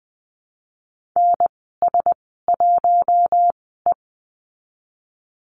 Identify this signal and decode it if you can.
Morse code, heard as a keyed tone, about 19 wpm, spaced as for 14 wpm, 700 Hz. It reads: NS1E
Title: 048 Lesson 9g - Number 4 - Call Signs 20wpm